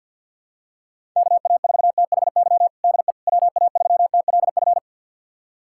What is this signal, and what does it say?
Morse code, heard as a keyed tone, about 50 wpm, 700 Hz. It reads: KA4THY DE WA3TLF
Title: USA Callsign DE Callsign 50wpm